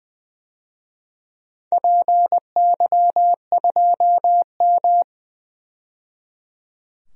Morse code, heard as a keyed tone, about 20 wpm, 700 Hz. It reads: PY2M
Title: DXCC-100 Call Signs - 4 Characters - 20wpm